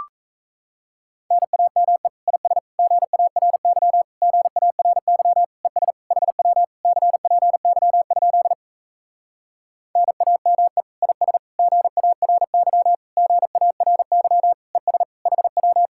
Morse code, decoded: NAME IS GARY GARY ES HW CPY? NAME IS GARY GARY ES HW

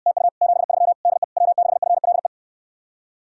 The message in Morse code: TU 73 DE W6FLE

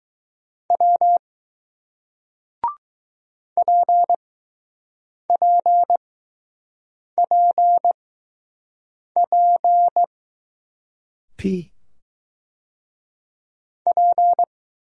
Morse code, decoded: W PPPP P

